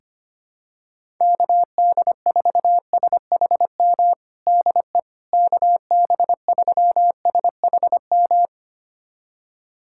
Morse code, heard as a keyed tone, about 25 wpm, 700 Hz. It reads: KD4SHM DE KB3SHM
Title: USA Callsign DE Callsign - 6 Characters - 25wpm